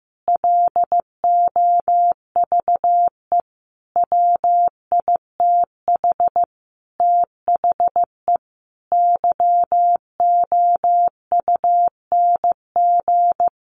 Morse code, heard as a keyed tone, about 15 wpm, 700 Hz. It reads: LOVE WITH THE YOUNG
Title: Sentences from Top 900 Words 15wpm